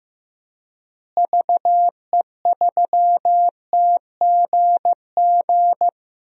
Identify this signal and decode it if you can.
Morse code, heard as a keyed tone, about 15 wpm, 700 Hz. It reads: VE3TGG
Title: DXCC-100 Call Signs - 6 Characters - Repeated 3x - 15wpm